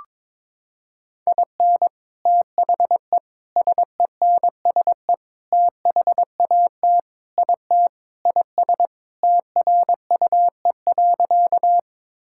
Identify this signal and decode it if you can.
Morse code, heard as a keyed tone, about 22 wpm, 700 Hz. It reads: IN THE SENSE THAT IT IS TRUE.
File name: practice / 22wpm / Sentences with 4-Word Phrases - 22wpm